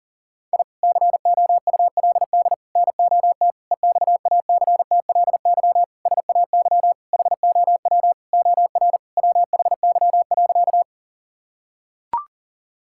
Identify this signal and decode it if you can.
Morse code, heard as a keyed tone, about 37 wpm, 700 Hz. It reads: I COULD NOT EXACTLY SAY HOW OR WHY.
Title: Sentences from Top 1000 Words - Review Part 6 - 40wpm